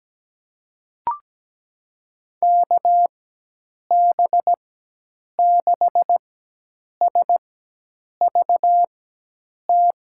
Morse code, heard as a keyed tone, about 17 wpm, spaced as for 8 wpm, 700 Hz. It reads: KB6SVT
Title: USA Call Signs - 6 Characters - 4x Character Spacing - 17wpm